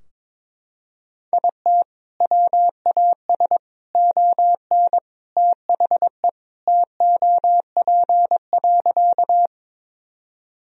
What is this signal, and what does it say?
Morse code, heard as a keyed tone, about 21 wpm, 700 Hz. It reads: IT WAS ON THE TOP.
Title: Sentences from Top 1000 Words - Review Part 2 - 22wpm